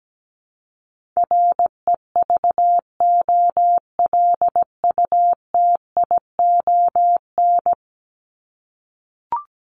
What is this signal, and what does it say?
Morse code, heard as a keyed tone, about 17 wpm, 700 Hz. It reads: REVOLUTION
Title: Top 1751-2000 Words 17wpm